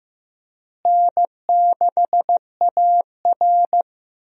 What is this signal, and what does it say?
Morse code, heard as a keyed tone, about 15 wpm, 700 Hz. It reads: N6AR
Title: USA Call Signs - 4 Characters - Repeated 2x - 15wpm